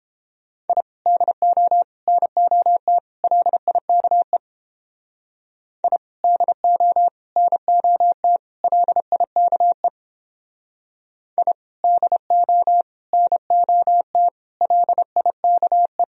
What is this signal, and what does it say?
Morse code, heard as a keyed tone, about 31 wpm, 700 Hz. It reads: I DO NOT LIKE I DO NOT LIKE I DO NOT LIKE